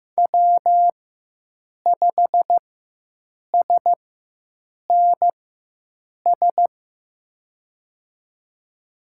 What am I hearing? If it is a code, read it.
Morse code, heard as a keyed tone, about 16 wpm, spaced as for 7 wpm, 700 Hz. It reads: W5SNS
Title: USA Call Signs - 5 Characters - 4x Character Spacing - 15wpm